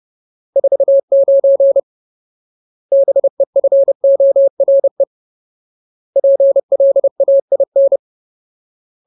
Morse code, heard as a keyed tone, about 30 wpm, 550 Hz. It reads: 49 BEFORE PLAIN
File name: Ultimate Word List / 30WPM Ultimate